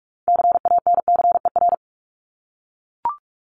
Morse code, heard as a keyed tone, about 44 wpm, 700 Hz. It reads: CANCER
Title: Top 1251-1500 Words 45wpm